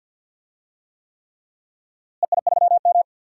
Morse code, heard as a keyed tone, about 47 wpm, 700 Hz. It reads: EI3M
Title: DXCC-100 Call Signs - 4 Characters - Repeated 3x - 50wpm